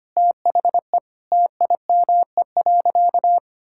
Morse code, heard as a keyed tone, about 25 wpm, 700 Hz. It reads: THE TIME.
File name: practice / 25wpm / Sentences with 9-Word Phrases - 25wpm